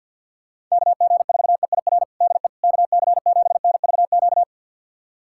Morse code, heard as a keyed tone, about 51 wpm, 700 Hz. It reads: KG4EIR DE KC7NVQ
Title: USA Callsign DE Callsign 50wpm